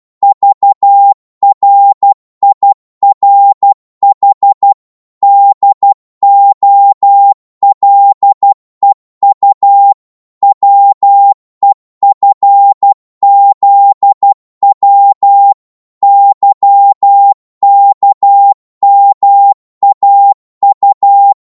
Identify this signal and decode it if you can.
Morse code, heard as a keyed tone, about 12 wpm, 800 Hz. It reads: VRIRH DOLEU WEFZW YKMAU
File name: Morse / ltr-12wpm